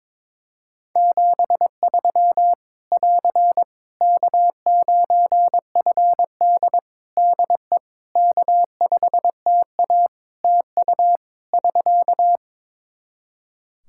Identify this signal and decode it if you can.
Morse code, heard as a keyed tone, about 21 wpm, 700 Hz. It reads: 73 <AR> K9FD DE K5TA TU <SK>
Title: K1USN SST QSO Practice 22wpm